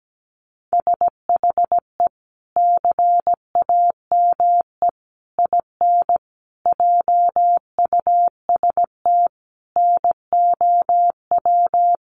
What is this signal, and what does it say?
Morse code, heard as a keyed tone, about 17 wpm, 700 Hz. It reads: SHE CAME IN JUST NOW
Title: Sentences from Top 300 Words 17wpm